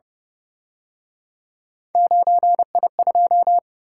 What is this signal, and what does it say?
Morse code, heard as a keyed tone, about 30 wpm, 700 Hz. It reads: 9I2
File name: Speed-Racing - Three Letter-Number Combinations - 25wpm